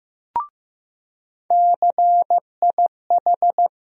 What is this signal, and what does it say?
Morse code, heard as a keyed tone, about 15 wpm, 700 Hz. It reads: CIH
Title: USA Call Sign Postfixes - 3 Letters 15wpm